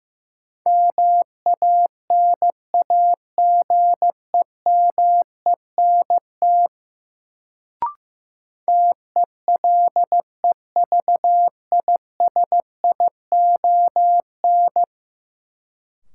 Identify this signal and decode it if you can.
Morse code, heard as a keyed tone, about 15 wpm, 700 Hz. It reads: MANAGEMENT TELEVISION
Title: Sentences from Top 1000 Words - Review Part 4 - New Word Drill - 15wpm